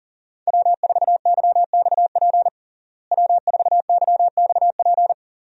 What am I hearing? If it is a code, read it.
Morse code, heard as a keyed tone, about 40 wpm, 700 Hz. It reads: W4YXP W4YXP
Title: USA Call Signs - 5 Characters - Repeated 3x - 40wpm